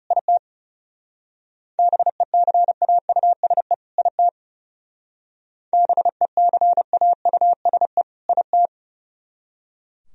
Morse code, heard as a keyed tone, about 37 wpm, 700 Hz. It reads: IT BECAUSE IT BECAUSE IT
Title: Speed-Racing - Sets of 2 Words 30wpm